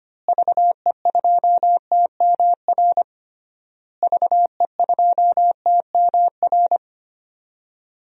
Morse code, heard as a keyed tone, about 25 wpm, 700 Hz. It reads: VE2TMR VE2TMR
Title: DXCC-100 Call Signs - 6 Characters - Repeated 3x - 25wpm